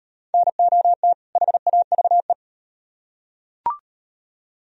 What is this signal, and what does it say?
Morse code, heard as a keyed tone, about 38 wpm, 700 Hz. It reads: NOT HAVE